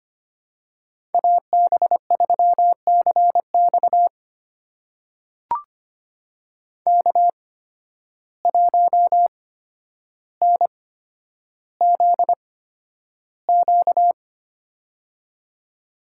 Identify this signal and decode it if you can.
Morse code, heard as a keyed tone, about 25 wpm, spaced as for 11 wpm, 700 Hz. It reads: AB3CX K1NZQ
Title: USA Call Signs - 5 Characters - 8x Character Spacing - 25wpm